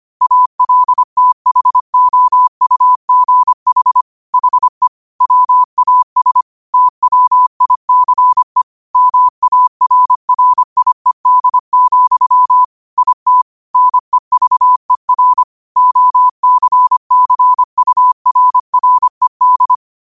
Morse code, heard as a keyed tone, about 25 wpm, 1000 Hz. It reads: ALTHOUGH HE WAS TWICE MARRIED, IT NEVER OCCURRED
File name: qotdfiles / 25wpm